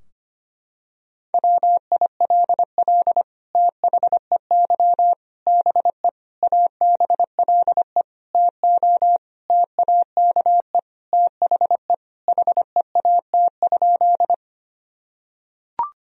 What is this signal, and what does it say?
Morse code, heard as a keyed tone, about 25 wpm, 700 Hz. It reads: WILL THEY BE ABLE TO TAKE THE HEAT?
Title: Sentences from Top 1000 Words - Review Part 3 - 25wpm